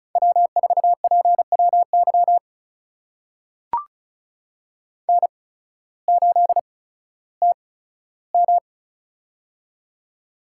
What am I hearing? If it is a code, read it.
Morse code, heard as a keyed tone, about 36 wpm, spaced as for 16 wpm, 700 Hz. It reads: W4PWY N8TM
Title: USA Call Signs - 8x Character Spacing - 35wpm